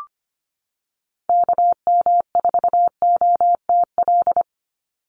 Morse code, heard as a keyed tone, about 25 wpm, 700 Hz. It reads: KM4OTL
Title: USA Call Signs 25wpm